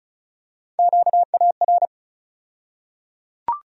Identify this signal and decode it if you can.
Morse code, heard as a keyed tone, about 34 wpm, 700 Hz. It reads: QAR
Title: USA Call Sign Postfixes - 3 Letters 35wpm